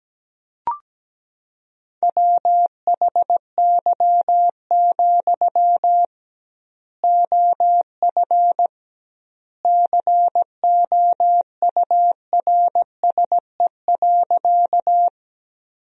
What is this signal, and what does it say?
Morse code, heard as a keyed tone, about 17 wpm, 700 Hz. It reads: WHY, OF COURSE.
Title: 2x Word-Spacing - Sentences from Top 500 Words - 17wpm